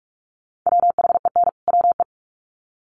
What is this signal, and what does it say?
Morse code, heard as a keyed tone, about 44 wpm, 700 Hz. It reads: WHEN WE
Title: Sets of 2 Words 45wpm